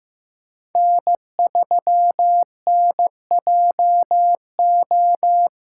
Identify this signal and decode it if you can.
Morse code, heard as a keyed tone, about 15 wpm, 700 Hz. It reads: N3NJO
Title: USA Call Signs - 5 Characters - 4x Character Spacing - 15wpm